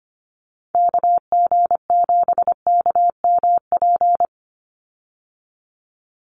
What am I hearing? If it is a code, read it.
Morse code, heard as a keyed tone, about 25 wpm, 700 Hz. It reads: KG7KMP
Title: USA Call Signs 25wpm